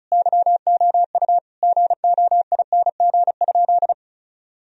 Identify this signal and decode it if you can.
Morse code, heard as a keyed tone, about 34 wpm, 700 Hz. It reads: YOU GOING?
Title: Sentences from Top 1000 Words - Review Part 2 - 35wpm